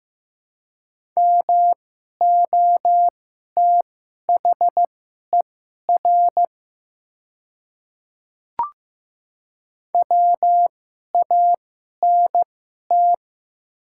Morse code, heard as a keyed tone, about 15 wpm, spaced as for 11 wpm, 700 Hz. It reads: MOTHER WANT
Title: 2x Character Spacing - Top 500 Words - 15wpm